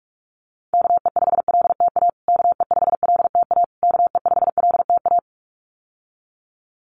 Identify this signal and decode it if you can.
Morse code, heard as a keyed tone, about 44 wpm, 700 Hz. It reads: KE5LTA KE5LTA KE5LTA
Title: USA Call Signs Repeated 3x 45wpm